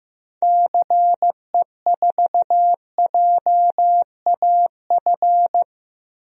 Morse code, heard as a keyed tone, about 16 wpm, 700 Hz. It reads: CE4JAF
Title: DXCC-100 Call Signs - 6 Characters - 15wpm